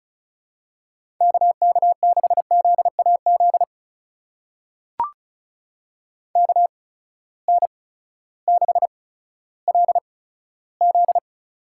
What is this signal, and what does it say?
Morse code, heard as a keyed tone, about 36 wpm, spaced as for 16 wpm, 700 Hz. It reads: KK6ZAZ KN6LZ
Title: USA Call Signs - 8x Character Spacing - 35wpm